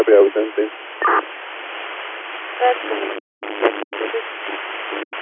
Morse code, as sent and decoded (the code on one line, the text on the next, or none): none